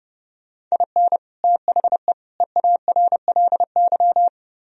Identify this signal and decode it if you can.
Morse code, heard as a keyed tone, about 35 wpm, 700 Hz. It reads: IN THE EARLY
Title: Speed-Racing - Sets of 3 Words 30wpm